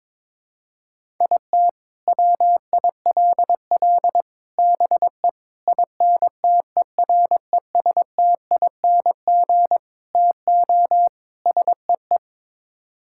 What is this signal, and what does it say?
Morse code, heard as a keyed tone, about 22 wpm, 700 Hz. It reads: IT WILL BE INTERESTING TO SEE
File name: practice / 22wpm / Sets of 6 Words - 22wpm